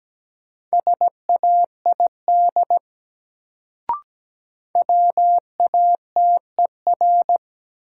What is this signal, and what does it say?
Morse code, heard as a keyed tone, about 17 wpm, 700 Hz. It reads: SAID WATER
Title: Sentences from Top 100 Words - New Word Drill - 17wpm